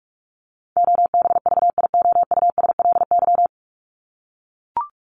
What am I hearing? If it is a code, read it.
Morse code, heard as a keyed tone, about 44 wpm, 700 Hz. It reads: OBVIOUSLY
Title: Top 1501-1750 Words 45wpm